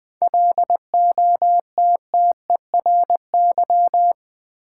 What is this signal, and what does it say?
Morse code, heard as a keyed tone, about 20 wpm, 700 Hz. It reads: LOTTERY
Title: 7 Letter Words - First 3 Letters Given - 20wpm